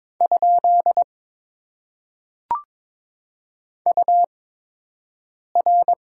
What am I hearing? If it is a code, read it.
Morse code, heard as a keyed tone, about 23 wpm, spaced as for 6 wpm, 700 Hz. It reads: ? UR
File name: Top 90 QSO Elements - 8x Character Spacing - 22wpm